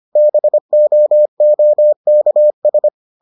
Morse code, heard as a keyed tone, about 25 wpm, 600 Hz. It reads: BOOKS